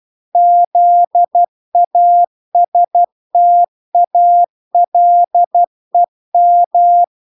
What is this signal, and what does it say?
Morse code, heard as a keyed tone, about 12 wpm, 700 Hz. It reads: ZASTALEM